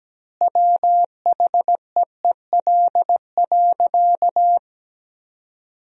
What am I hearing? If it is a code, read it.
Morse code, heard as a keyed tone, about 17 wpm, 700 Hz. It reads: WHEEL.
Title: 3x Word-Spacing - Sentences from Top 500 Words - 17wpm